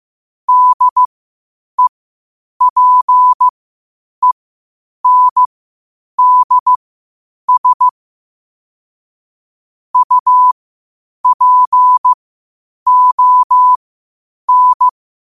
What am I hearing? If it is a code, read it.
Morse code, heard as a keyed tone, about 15 wpm, spaced as for 8 wpm, 1000 Hz. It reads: DEPENDS UPON
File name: qotdfiles / 8wpm